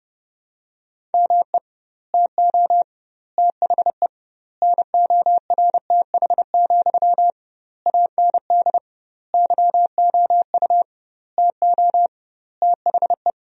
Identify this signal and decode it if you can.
Morse code, heard as a keyed tone, about 30 wpm, 700 Hz. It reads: ME TO THE NORTH, AND YOU TO THE
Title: 2x Word-Spacing - Sentences from Top 400 Words - 30wpm